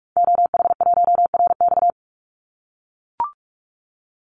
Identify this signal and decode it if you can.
Morse code, heard as a keyed tone, about 44 wpm, 700 Hz. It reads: OH1RX
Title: Call Signs - 45wpm